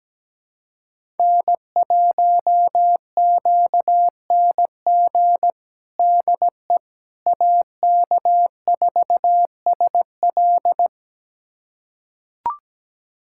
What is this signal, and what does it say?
Morse code, heard as a keyed tone, about 17 wpm, 700 Hz. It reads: N1QNG DE AK4SL
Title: USA Callsign DE Callsign - 5 Characters - 17wpm